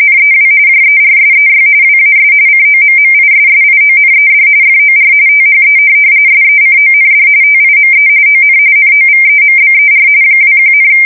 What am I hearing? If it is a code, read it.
RTTY, about 45.45 baud. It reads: NEW JERSEY WAS COMMISSIONED FOR THE SECOND TIME ON NOVEMBER 21,
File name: fsk170 / USS-New-Jersey